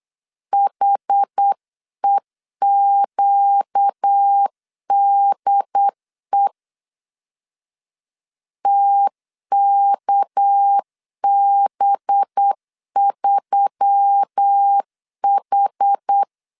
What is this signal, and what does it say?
Morse code, heard as a keyed tone, about 8 wpm, 800 Hz. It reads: HEQDE TKB3H
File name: 8wpm